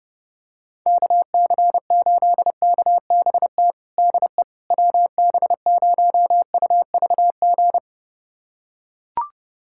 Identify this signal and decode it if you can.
Morse code, heard as a keyed tone, about 30 wpm, 700 Hz. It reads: KC8KBT DE WB0UVG
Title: USA Callsign DE Callsign - 6 Characters - 30wpm